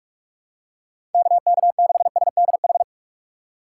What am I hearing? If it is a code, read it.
Morse code, heard as a keyed tone, about 43 wpm, 690 Hz. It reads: KK6SDH
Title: USA Call Signs - 6 Characters - 3x Character Spacing - 45wpm